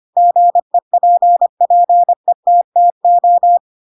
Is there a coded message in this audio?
Yes: Morse code from 0.2 to 3.6 s, heard as a keyed tone, about 25 wpm, 700 Hz. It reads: GEPPETTO